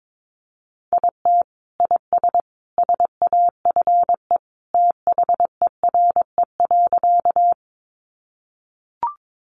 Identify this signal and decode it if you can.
Morse code, heard as a keyed tone, about 22 wpm, 700 Hz. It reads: IT IS SAFE THERE.
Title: Easy Phrases 22wpm